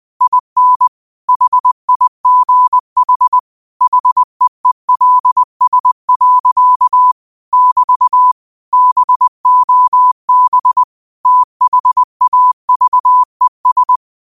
Morse code, heard as a keyed tone, about 20 wpm, 1000 Hz. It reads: IN HIGH HEELS. <BT> BOB THAVES
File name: qotdfiles / 20wpm